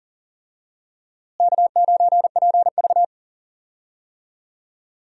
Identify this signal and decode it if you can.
Morse code, heard as a keyed tone, about 40 wpm, 700 Hz. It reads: K9PV
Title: USA Call Signs - 4 Characters - Repeated 2x - 40wpm